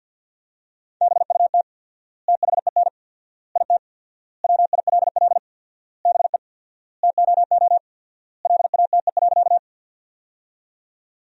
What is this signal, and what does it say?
Morse code, heard as a keyed tone, about 50 wpm, 700 Hz. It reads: BUT THEN IT WILL BE TOO LATE.
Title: 4x Word-Spacing - Sentences from Top 300 Words - 50wpm